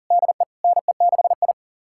Morse code, heard as a keyed tone, about 37 wpm, 700 Hz. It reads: DE NE6I
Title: USA Callsign DE Callsign - 4 Characters - 40wpm